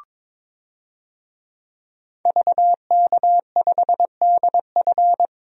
Morse code, heard as a keyed tone, about 22 wpm, 700 Hz. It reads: VK5DF
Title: DXCC-100 Call Signs - 5 Characters - 22wpm